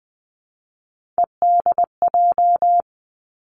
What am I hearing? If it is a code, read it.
Morse code, heard as a keyed tone, about 20 wpm, 700 Hz. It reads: EDJ
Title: Three Challenging Letter-Number Combinations 20wpm